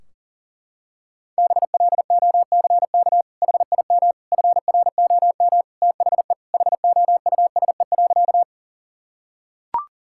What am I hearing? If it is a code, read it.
Morse code, heard as a keyed tone, about 40 wpm, 700 Hz. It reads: BLOCK HIM FROM THE HOUSE.
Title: Sentences from Top 1600 Words 40wpm